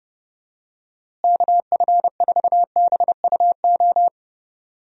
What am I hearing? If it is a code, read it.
Morse code, heard as a keyed tone, about 32 wpm, 700 Hz. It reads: KF4BUO
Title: USA Call Signs - 6 Characters - Repeated 3x - 30wpm